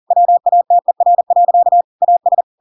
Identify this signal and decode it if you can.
Morse code, heard as a keyed tone, about 40 wpm, 700 Hz. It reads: WATER. AS
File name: Pinocchio / Pinocchio ch5 40 40 ik1hge